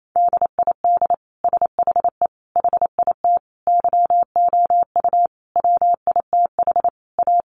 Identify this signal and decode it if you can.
Morse code, heard as a keyed tone, about 28 wpm, 700 Hz. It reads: DID SHE HIT YOU WITH A
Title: Sentences from Top 1400 Words 28wpm